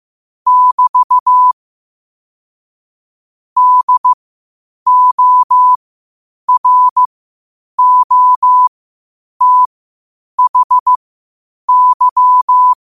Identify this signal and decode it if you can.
Morse code, heard as a keyed tone, about 15 wpm, spaced as for 8 wpm, 1000 Hz. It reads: <BT> DOROTHY
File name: qotdfiles / 8wpm